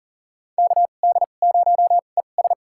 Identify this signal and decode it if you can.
Morse code, heard as a keyed tone, about 40 wpm, spaced as for 29 wpm, 700 Hz. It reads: KD0ES
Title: USA Call Signs - 5 Characters - 2x Character Spacing - 40wpm